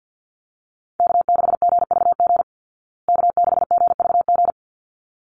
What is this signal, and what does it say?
Morse code, heard as a keyed tone, about 50 wpm, 700 Hz. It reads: K6GVG K6GVG
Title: Call Signs Repeated 3x - 50wpm